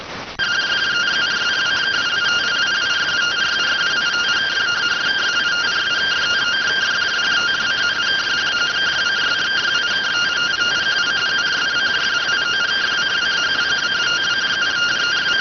Two-tone teleprinter signal, about 170 Hz shift.